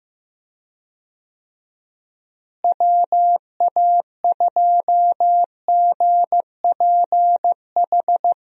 Morse code, heard as a keyed tone, about 15 wpm, 700 Hz. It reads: WA2GPH
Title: USA Call Signs - 6 Characters - Repeated 3x - 15wpm